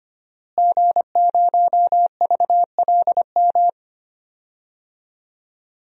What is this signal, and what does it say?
Morse code, heard as a keyed tone, about 25 wpm, 700 Hz. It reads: G0VLM
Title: DXCC-100 Call Signs - 5 Characters - Repeated 2x - 25wpm